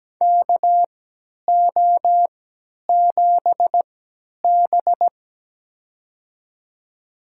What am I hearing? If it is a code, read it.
Morse code, heard as a keyed tone, about 17 wpm, spaced as for 10 wpm, 700 Hz. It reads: KO7B
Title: USA Call Signs - 4 Characters - 3x Character Spacing - 17wpm